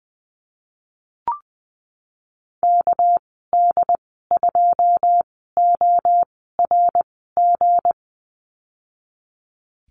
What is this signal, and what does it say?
Morse code, heard as a keyed tone, about 20 wpm, spaced as for 15 wpm, 700 Hz. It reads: KD2ORG
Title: 219 Lesson 35g - Number 0 - Call Signs 20wpm